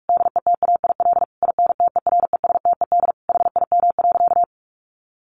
Morse code, heard as a keyed tone, about 44 wpm, 700 Hz. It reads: DETAIL INTERESTED HIM.
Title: Sentences from Top 1500 Words 45wpm